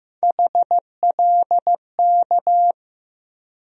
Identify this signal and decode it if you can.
Morse code, heard as a keyed tone, about 15 wpm, 700 Hz. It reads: HLK
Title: USA Call Sign Postfixes - 3 Letters 15wpm